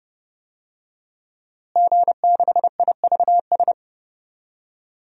Morse code, heard as a keyed tone, about 30 wpm, 700 Hz. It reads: G6IVS
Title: DXCC-100 Call Signs - 5 Characters - Repeated 3x - 30wpm